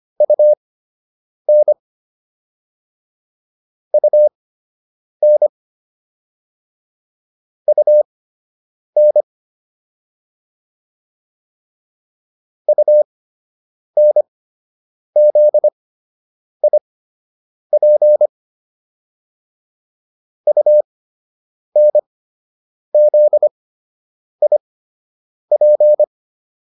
Morse code, heard as a keyed tone, about 25 wpm, spaced as for 8 wpm, 600 Hz. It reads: UN UN UN UNZIP UNZIP